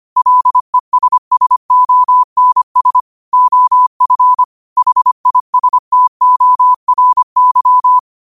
Morse code, heard as a keyed tone, about 25 wpm, 1000 Hz. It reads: LESSONS OF HISTORY
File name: qotdfiles / 25wpm